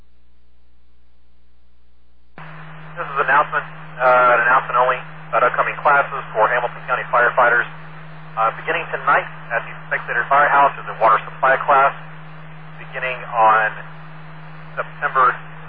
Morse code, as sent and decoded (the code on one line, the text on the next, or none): none